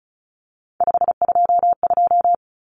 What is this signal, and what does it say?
Morse code, heard as a keyed tone, about 34 wpm, 700 Hz. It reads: 522
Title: Common Cut Numbers - 3-Digit 35wpm